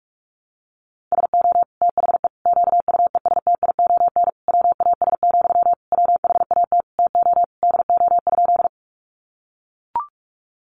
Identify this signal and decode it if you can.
Morse code, heard as a keyed tone, about 44 wpm, 700 Hz. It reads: SO THE QUESTION WAS, WHAT TO DO?
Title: Sentences from Top 400 Words 45wpm